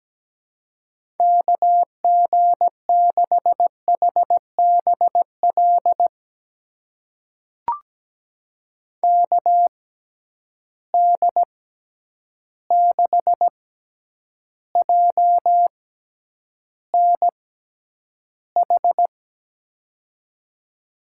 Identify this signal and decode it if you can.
Morse code, heard as a keyed tone, about 17 wpm, spaced as for 9 wpm, 700 Hz. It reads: KG6HBL KD6JNH
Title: USA Call Signs - 6x Character Spacing - 17wpm